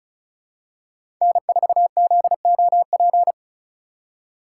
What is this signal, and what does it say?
Morse code, heard as a keyed tone, about 34 wpm, 700 Hz. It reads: N4ZOP
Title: USA Call Signs - 5 Characters - 4x Character Spacing - 35wpm